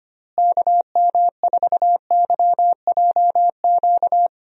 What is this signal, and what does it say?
Morse code, heard as a keyed tone, about 25 wpm, 700 Hz. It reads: KM4YJQ
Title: USA Call Signs - 6 Characters - Repeated 3x - 25wpm